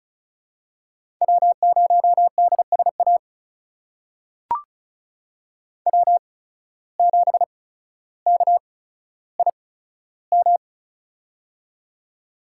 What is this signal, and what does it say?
Morse code, heard as a keyed tone, about 36 wpm, spaced as for 15 wpm, 700 Hz. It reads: W0DSA W7KIM
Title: USA Call Signs - 5 Characters - 8x Character Spacing - 35wpm